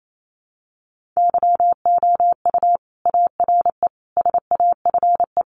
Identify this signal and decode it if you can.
Morse code, heard as a keyed tone, about 28 wpm, 700 Hz. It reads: YOU ARE SAFE
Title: Sentences from Top 900 Words 28wpm